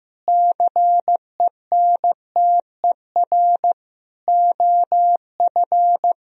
Morse code, heard as a keyed tone, about 15 wpm, 700 Hz. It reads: CENTER OF
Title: Sentences from Top 1000 Words - Review Part 2 - 15wpm